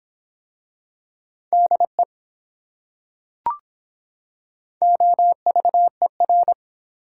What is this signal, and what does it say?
Morse code, heard as a keyed tone, about 23 wpm, 700 Hz. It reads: DE OVER